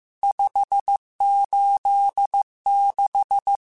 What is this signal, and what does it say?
Morse code, heard as a keyed tone, about 15 wpm, 800 Hz. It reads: 586
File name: RSTs / RST-mp3s / Various RST @13